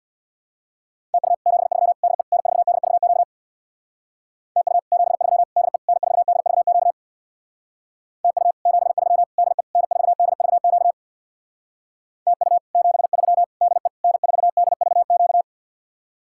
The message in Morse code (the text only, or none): TU 73 DE N4DVQ TU 73 DE N4DVQ TU 73 DE N4DVQ TU 73 DE N4DVQ